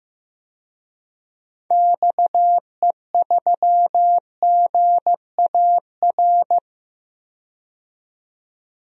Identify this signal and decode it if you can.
Morse code, heard as a keyed tone, about 15 wpm, 700 Hz. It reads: XE3GAR
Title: DXCC-100 Call Signs - 6 Characters - 15wpm